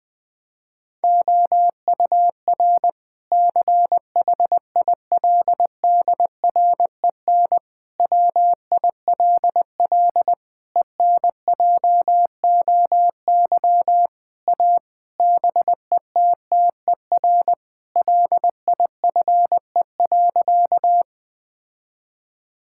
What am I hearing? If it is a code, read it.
Morse code, heard as a keyed tone, about 21 wpm, 700 Hz. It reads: OUR CHILDREN WILL ENJOY A BETTER LIFE.
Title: Sentences from Top 1000 Words - Review Part 5 - 20wpm